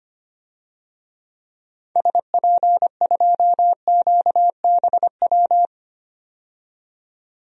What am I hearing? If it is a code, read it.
Morse code, heard as a keyed tone, about 25 wpm, 700 Hz. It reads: SP2QBW
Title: DXCC-100 Call Signs - 6 Characters - 25wpm